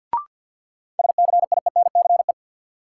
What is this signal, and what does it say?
Morse code, heard as a keyed tone, about 52 wpm, 700 Hz. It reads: SCIENCE
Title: Sentences from Top 1400 Words - New Word Drill Alt - 50wpm